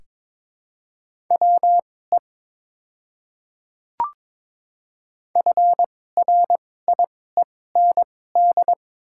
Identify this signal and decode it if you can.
Morse code, heard as a keyed tone, about 21 wpm, spaced as for 16 wpm, 700 Hz. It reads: WE FRIEND
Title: 2x Character Spacing - Top 500 Words - 22wpm